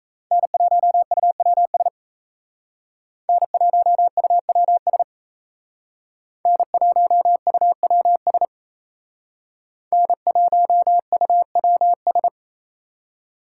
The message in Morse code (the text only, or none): N1UWS N1UWS N1UWS N1UWS